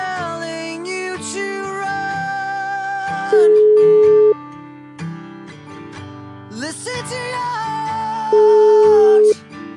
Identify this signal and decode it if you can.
Ringback tone (Europe, 425 Hz).